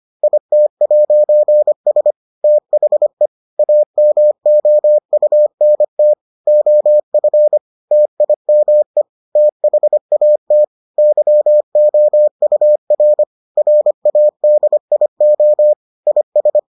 Morse code, heard as a keyed tone, about 25 wpm, 600 Hz. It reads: IT'S THE AMOUNT OF TIME THAT YOUR RADIO IS